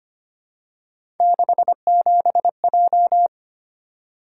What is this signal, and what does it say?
Morse code, heard as a keyed tone, about 25 wpm, 700 Hz. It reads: 67J
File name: Three Letter-Number Combinations - 25wpm